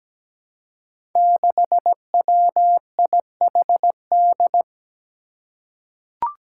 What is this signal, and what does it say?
Morse code, heard as a keyed tone, about 17 wpm, 700 Hz. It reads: 6WIHD
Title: Five Challenging Letter-Number Combinations 17wpm